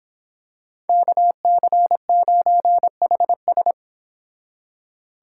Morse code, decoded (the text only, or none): KC9HS